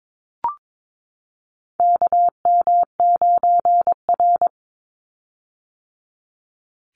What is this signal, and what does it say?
Morse code, heard as a keyed tone, about 22 wpm, 700 Hz. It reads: KM9R
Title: Call Signs - 22wpm